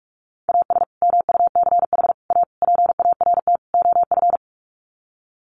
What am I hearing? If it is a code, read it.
Morse code, heard as a keyed tone, about 44 wpm, 700 Hz. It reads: AS MUCH A PART OF
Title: Sets of 5 Words Encore2 45wpm